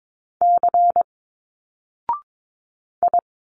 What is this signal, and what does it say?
Morse code, heard as a keyed tone, about 22 wpm, 700 Hz. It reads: C I
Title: Single Letters 22wpm